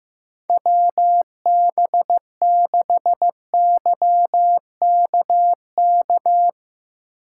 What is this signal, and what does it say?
Morse code, heard as a keyed tone, about 15 wpm, 700 Hz. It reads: WB6YKK
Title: USA Call Signs - 8x Character Spacing - 15wpm